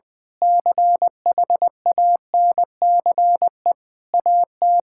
Morse code, heard as a keyed tone, about 20 wpm, 700 Hz. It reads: CHANCE AT
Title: Difficult Sentences with Hints - 20wpm